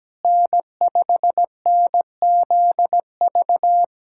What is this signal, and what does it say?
Morse code, heard as a keyed tone, about 17 wpm, 700 Hz. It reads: N5NZV
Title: USA Callsign DE Callsign - 5 Characters - 17wpm